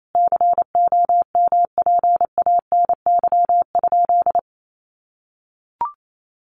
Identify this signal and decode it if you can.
Morse code, heard as a keyed tone, about 28 wpm, 700 Hz. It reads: COMPANY?
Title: Sentences from Top 1300 Words 28wpm